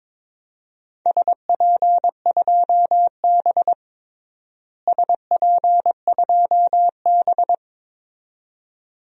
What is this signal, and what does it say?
Morse code, heard as a keyed tone, about 21 wpm, 700 Hz. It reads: SP2B SP2B
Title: DXCC-100 Call Signs - 4 Characters - Repeated 2x - 22wpm